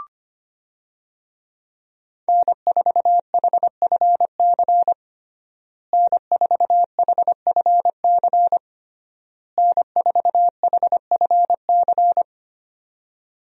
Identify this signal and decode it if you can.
Morse code, heard as a keyed tone, about 25 wpm, 700 Hz. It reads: N4HFC N4HFC N4HFC
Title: USA Call Signs - 5 Characters - Repeated 3x - 25wpm